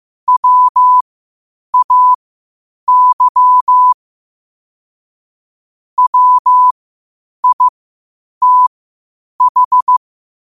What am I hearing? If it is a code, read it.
Morse code, heard as a keyed tone, about 15 wpm, spaced as for 8 wpm, 1000 Hz. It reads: WAY WITH
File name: qotdfiles / 8wpm